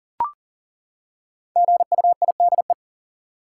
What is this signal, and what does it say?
Morse code, heard as a keyed tone, about 40 wpm, 700 Hz. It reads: GUIDE
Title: Five Letter Words - 40wpm